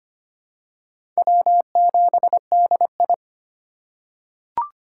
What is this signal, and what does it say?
Morse code, heard as a keyed tone, about 25 wpm, 700 Hz. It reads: W7DI
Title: USA Call Signs - 4 Characters - 3x Character Spacing - 25wpm